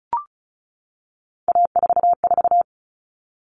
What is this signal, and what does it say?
Morse code, heard as a keyed tone, about 34 wpm, 700 Hz. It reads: A44
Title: All Cut Numbers - 3-Digit 35wpm